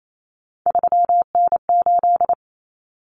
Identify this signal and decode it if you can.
Morse code, heard as a keyed tone, about 28 wpm, 700 Hz. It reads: 3N8
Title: Common Cut Numbers - 3-Digit 28wpm